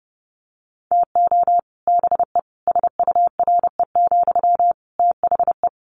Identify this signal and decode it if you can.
Morse code, heard as a keyed tone, about 30 wpm, 700 Hz. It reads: TO BE SURE, THE
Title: Sentences from Top 1000 Words 30wpm